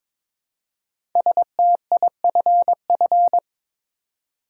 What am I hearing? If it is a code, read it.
Morse code, heard as a keyed tone, about 22 wpm, 700 Hz. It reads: STIFF
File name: 5 Letter Words - First 3 Letters Given - 22wpm